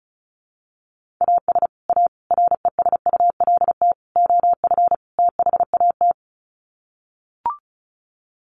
Morse code, heard as a keyed tone, about 34 wpm, 700 Hz. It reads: AS A RESULT OF THAT
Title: Sets of 5 Words Encore 35wpm